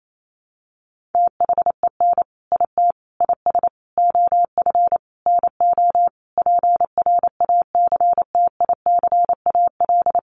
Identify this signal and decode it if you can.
Morse code, heard as a keyed tone, about 28 wpm, 700 Hz. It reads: THEN IT IS OF NO PRACTICAL
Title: Sentences from Top 1400 Words 28wpm